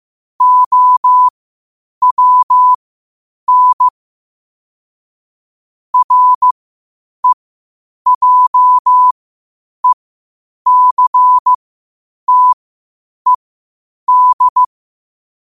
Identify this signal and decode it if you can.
Morse code, heard as a keyed tone, about 15 wpm, spaced as for 8 wpm, 1000 Hz. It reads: OWN REJECTED
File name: qotdfiles / 8wpm